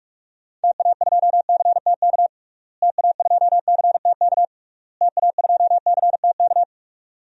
Morse code, heard as a keyed tone, about 46 wpm, 700 Hz. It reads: TA2CTK TA2CTK TA2CTK
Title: DXCC-100 Call Signs - 6 Characters - Repeated 3x - 45wpm